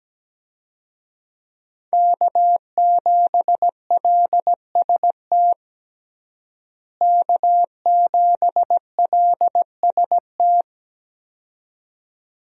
Morse code, heard as a keyed tone, about 17 wpm, 700 Hz. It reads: K7LST K7LST
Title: USA Call Signs - 5 Characters - Repeated 2x - 17wpm